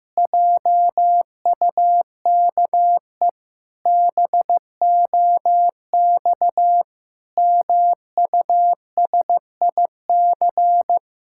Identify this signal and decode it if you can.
Morse code, heard as a keyed tone, about 15 wpm, 700 Hz. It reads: JUKE BOX MUSIC
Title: Pangram Challenge v2 - 15wpm